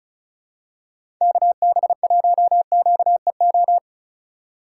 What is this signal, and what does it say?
Morse code, heard as a keyed tone, about 36 wpm, 700 Hz. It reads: KB1QEO
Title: USA Call Signs - 6 Characters - 4x Character Spacing - 35wpm